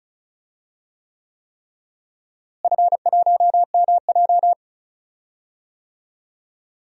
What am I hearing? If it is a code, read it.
Morse code, heard as a keyed tone, about 34 wpm, 700 Hz. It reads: F1MJ
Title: DXCC-100 Call Signs - 4 Characters - Repeated 3x - 35wpm